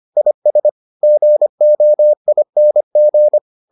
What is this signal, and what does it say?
Morse code, heard as a keyed tone, about 25 wpm, 600 Hz. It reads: IS GOING